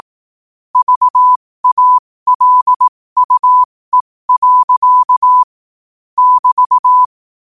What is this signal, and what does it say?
Morse code, heard as a keyed tone, about 18 wpm, spaced as for 15 wpm, 1000 Hz. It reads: VALUE. <BT>
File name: qotdfiles / 15wpm